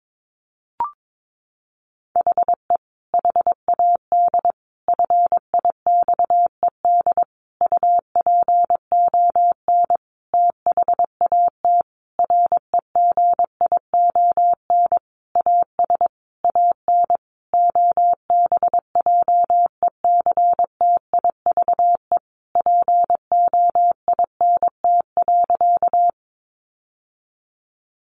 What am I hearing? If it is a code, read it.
Morse code, heard as a keyed tone, about 22 wpm, 700 Hz. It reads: HE HAD FIXED UPON THAT REGION AS AN OBJECTIVE POINT.
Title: Sentences from Top 1500 Words 22wpm